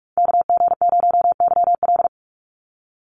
Morse code, heard as a keyed tone, about 44 wpm, 700 Hz. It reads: KG0YL
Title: Call Signs - 45wpm